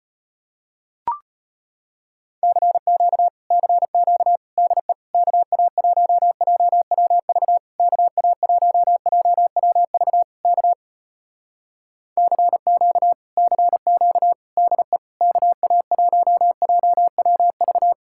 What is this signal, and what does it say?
Morse code, heard as a keyed tone, about 33 wpm, 700 Hz. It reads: CQ CQ DE KA1JWV KA1JWV K CQ CQ DE KA1JWV